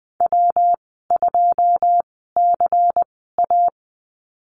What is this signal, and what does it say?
Morse code, heard as a keyed tone, about 19 wpm, spaced as for 14 wpm, 700 Hz. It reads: W2CA
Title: 173 Lesson 28g - Number 9 - Call Signs 20wpm